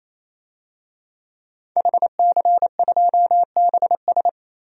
Morse code, heard as a keyed tone, about 28 wpm, 700 Hz. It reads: HC2BS
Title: DXCC-100 Call Signs - 5 Characters - 28wpm